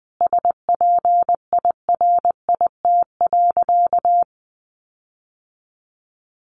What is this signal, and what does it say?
Morse code, heard as a keyed tone, about 20 wpm, 700 Hz. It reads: SPIRIT.
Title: Sentences from Top 1400 Words 20wpm